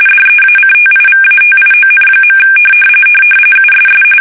RTTY, about 45.45 baud. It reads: DEACTIVATION. USS
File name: fsk850 / battleshipnj